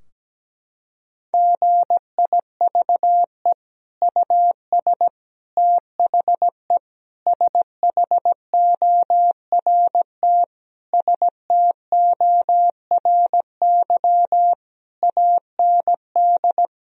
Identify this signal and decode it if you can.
Morse code, heard as a keyed tone, about 17 wpm, 700 Hz. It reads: GIVE US THE SHORT STORY AND
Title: Sentences from Top 1000 Words - Review Part 4 - 17wpm